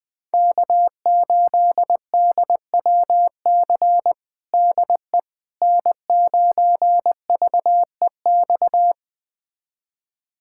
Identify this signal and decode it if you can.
Morse code, heard as a keyed tone, about 21 wpm, 700 Hz. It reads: K8DWC DE N9VEX
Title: USA Callsign DE Callsign - 5 Characters - 20wpm